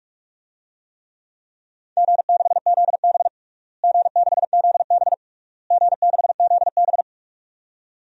Morse code, heard as a keyed tone, about 46 wpm, 690 Hz. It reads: G6ZB G6ZB G6ZB
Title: DXCC-100 Call Signs - 4 Characters - Repeated 3x - 45wpm